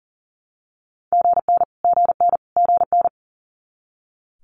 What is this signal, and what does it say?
Morse code, heard as a keyed tone, about 38 wpm, 700 Hz. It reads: GN GN GN
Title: Multi-letter Phonemes Repeated 3x 40wpm